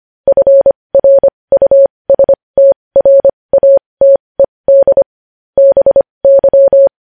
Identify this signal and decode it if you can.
Morse code, heard as a keyed tone, about 25 wpm, spaced as for 20 wpm, 550 Hz. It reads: FRUSTRATED BY
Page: news-headlines-in-morse-code-at-20-wpm